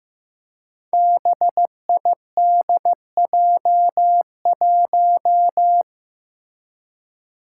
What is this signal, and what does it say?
Morse code, heard as a keyed tone, about 15 wpm, 700 Hz. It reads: BIDJ1
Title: Five Challenging Letter-Number Combinations 15wpm